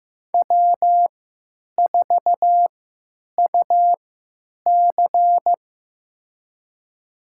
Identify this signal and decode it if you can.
Morse code, heard as a keyed tone, about 15 wpm, spaced as for 9 wpm, 700 Hz. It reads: W4UC
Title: USA Call Signs - 4 Characters - 3x Character Spacing - 15wpm